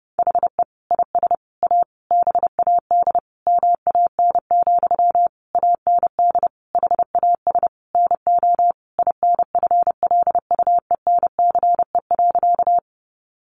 Morse code, heard as a keyed tone, about 30 wpm, 700 Hz. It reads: HE IS A BAD MAN, AND HAS NO INFLUENCE.
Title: Sentences from Top 1100 Words 30wpm